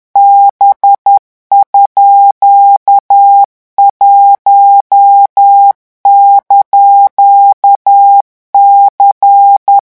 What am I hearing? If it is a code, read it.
Morse code, heard as a keyed tone, about 10 wpm, 800 Hz. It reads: B_1)C
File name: morse mp3 / revision17-vitesse 10 mots